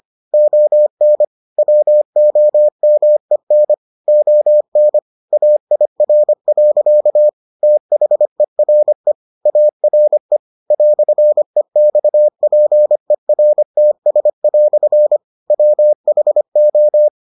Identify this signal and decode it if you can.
Morse code, heard as a keyed tone, about 25 wpm, 600 Hz. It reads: ON WOMEN ON AIR. THERE ARE "EXPERTS" WHO